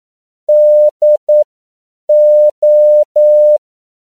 Morse code, heard as a keyed tone, about 9 wpm, spaced as for 7 wpm, 600 Hz. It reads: DO
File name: CW / 10PPM